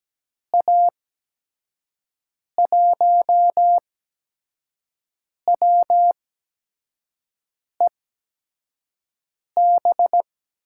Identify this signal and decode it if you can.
Morse code, heard as a keyed tone, about 17 wpm, spaced as for 5 wpm, 700 Hz. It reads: A1WEB
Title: USA Call Signs - 6 Characters - 8x Character Spacing - 17wpm